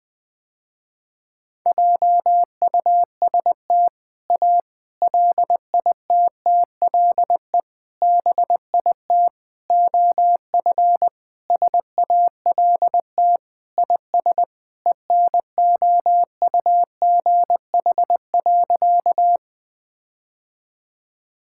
Morse code, heard as a keyed tone, about 20 wpm, 690 Hz. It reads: JUST A LITTLE BIT OF SALT IS ENOUGH.
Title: Sentences with 4-Word Phrases - Spoken Beginning - 20wpm